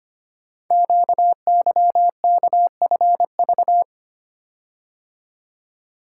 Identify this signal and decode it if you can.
Morse code, heard as a keyed tone, about 25 wpm, 700 Hz. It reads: QYKFV
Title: Code Groups - 5 Letters - 25wpm